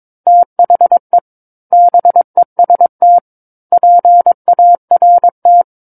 Morse code, heard as a keyed tone, about 22 wpm, 700 Hz. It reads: THE BEST PART
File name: Book One / 13-Chapter 13 - 22 WPM